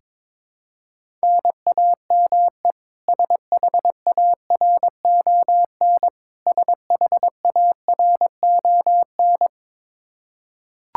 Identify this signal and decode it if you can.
Morse code, heard as a keyed tone, about 22 wpm, 700 Hz. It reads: NAME SHARON SHARON
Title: Practice QSO - Exchange 1 - Beginning - 22wpm